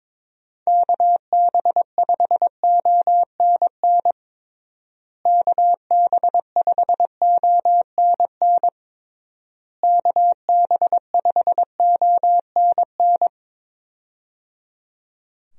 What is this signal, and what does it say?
Morse code, heard as a keyed tone, about 22 wpm, 700 Hz. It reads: KB5ONN KB5ONN KB5ONN
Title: USA Call Signs - 6 Characters - Repeated 3x - 22wpm